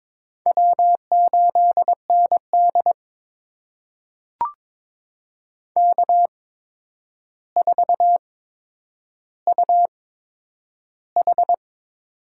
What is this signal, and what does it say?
Morse code, heard as a keyed tone, about 22 wpm, spaced as for 9 wpm, 700 Hz. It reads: W8ND K4UH
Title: USA Call Signs - 4 Characters - 8x Character Spacing - 22wpm